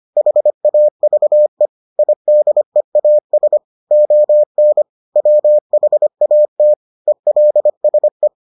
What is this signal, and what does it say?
Morse code, heard as a keyed tone, about 25 wpm, 600 Hz. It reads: HAVE IDEAS ON WHAT ELSE